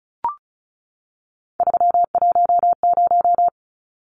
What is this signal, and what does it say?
Morse code, heard as a keyed tone, about 34 wpm, 700 Hz. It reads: 310
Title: Numbers - 3-Digit 35wpm